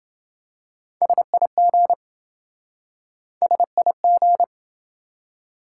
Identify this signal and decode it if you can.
Morse code, heard as a keyed tone, about 27 wpm, 700 Hz. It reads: SIG SIG